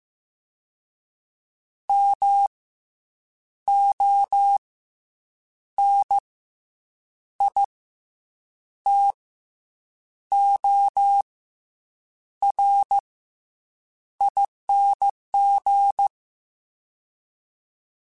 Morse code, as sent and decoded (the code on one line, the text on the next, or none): -- / --- / -. / .. / - / --- / .-. / .. -. --.
M O N I T O R ING